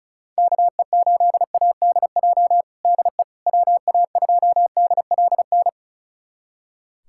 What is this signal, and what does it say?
Morse code, heard as a keyed tone, about 34 wpm, 700 Hz. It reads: KE8ADJ DE WA2DLN
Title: USA Callsign DE Callsign - 6 Characters - 35wpm